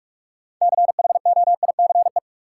Morse code, heard as a keyed tone, about 41 wpm, 700 Hz. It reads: CHOICE
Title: Sentences from Top 900 Words - New Word Drill - 45wpm